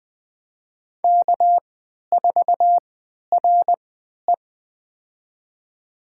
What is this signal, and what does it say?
Morse code, heard as a keyed tone, about 21 wpm, spaced as for 12 wpm, 700 Hz. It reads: K4RE
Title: USA Call Signs - 4 Characters - 3x Character Spacing - 20wpm